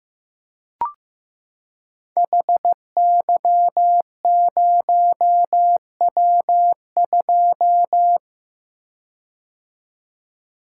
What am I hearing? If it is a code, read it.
Morse code, heard as a keyed tone, about 15 wpm, 700 Hz. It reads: HY0W2
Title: Code Groups - 5 Letter:Numbers - 15wpm